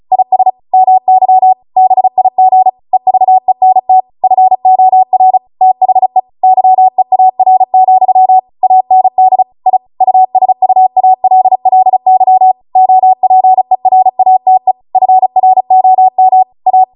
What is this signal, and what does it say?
Morse code, heard as a keyed tone, about 36 wpm, 750 Hz. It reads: IS MY BIG EVENT FOR THE YEAR, AND I USUALLY OPERATE FROM A